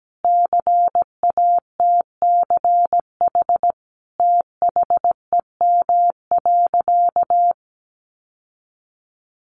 Morse code, heard as a keyed tone, about 17 wpm, 700 Hz. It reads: CATCH THEM.
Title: Sentences from Top 1500 Words 17wpm